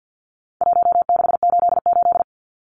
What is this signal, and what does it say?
Morse code, heard as a keyed tone, about 50 wpm, 700 Hz. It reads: 1688
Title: Numbers - Contest Style - 4-Digit 50wpm